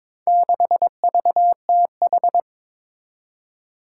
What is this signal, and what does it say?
Morse code, heard as a keyed tone, about 22 wpm, 700 Hz. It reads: 6VTH
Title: Four Challenging Letter-Number Combinations 22wpm